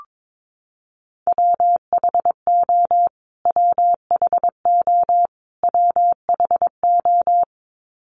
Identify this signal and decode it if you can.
Morse code, heard as a keyed tone, about 22 wpm, 700 Hz. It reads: WHO WHO WHO
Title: Top 200 Words Repeated 22wpm